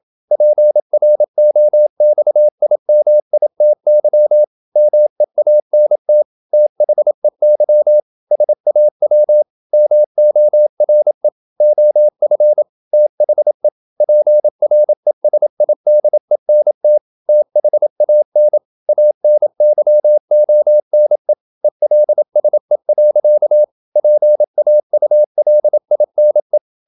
Morse code, heard as a keyed tone, about 27 wpm, 600 Hz. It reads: PROXIMITY MEANT THEY SAW MORE OF THE PRESIDENT THAN ANYONE ELSE. PAULINE
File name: Never 0027wpm